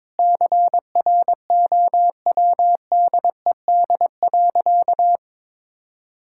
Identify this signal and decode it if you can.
Morse code, heard as a keyed tone, about 22 wpm, 700 Hz. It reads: CROWDED.
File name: Sentences with 7-Word Phrases - 22wpm